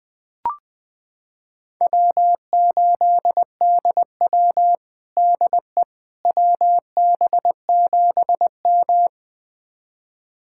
Morse code, heard as a keyed tone, about 21 wpm, 700 Hz. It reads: W8DW DE WB7M
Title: USA Callsign DE Callsign - 4 Characters - 20wpm